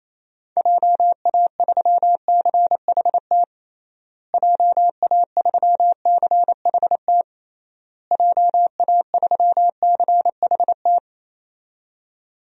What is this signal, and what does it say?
Morse code, heard as a keyed tone, about 28 wpm, 700 Hz. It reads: JA3CHT JA3CHT JA3CHT
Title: DXCC-100 Call Signs - 6 Characters - Repeated 3x - 28wpm